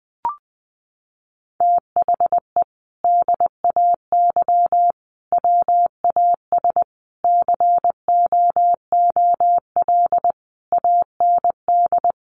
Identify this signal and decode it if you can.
Morse code, heard as a keyed tone, about 20 wpm, 700 Hz. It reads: THE DAY WAS COOL AND
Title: Sentences from Top 1200 Words 20wpm